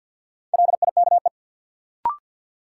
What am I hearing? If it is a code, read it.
Morse code, heard as a keyed tone, about 48 wpm, 700 Hz. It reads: LIKE
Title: Sentences from Top 100 Words - New Word Drill - 50wpm